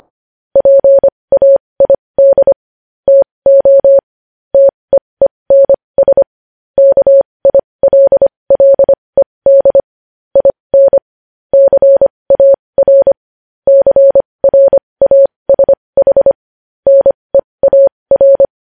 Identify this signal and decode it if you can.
Morse code, heard as a keyed tone, about 25 wpm, spaced as for 20 wpm, 550 Hz. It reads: PAID TO TEENS KILLED IN CAR CRASH NEAR
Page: News Headlines in Morse Code at 20 WPM